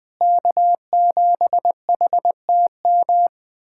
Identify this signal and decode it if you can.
Morse code, heard as a keyed tone, about 20 wpm, 700 Hz. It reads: K7HTM
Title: POTA QSO Practice - 20wpm